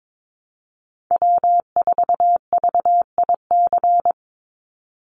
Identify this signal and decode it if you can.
Morse code, heard as a keyed tone, about 22 wpm, 700 Hz. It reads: W4VIC
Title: Call Signs Repeated 3x - 22wpm